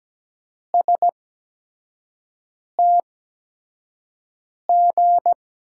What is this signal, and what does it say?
Morse code, heard as a keyed tone, about 17 wpm, spaced as for 5 wpm, 700 Hz. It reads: STG
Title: USA Call Signs - 8x Character Spacing - 17wpm